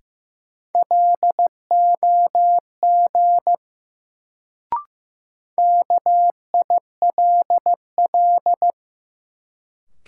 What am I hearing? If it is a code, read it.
Morse code, heard as a keyed tone, about 15 wpm, 700 Hz. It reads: LOG KILL
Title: Sentences from Top 1400 Words - New Word Drill - 15wpm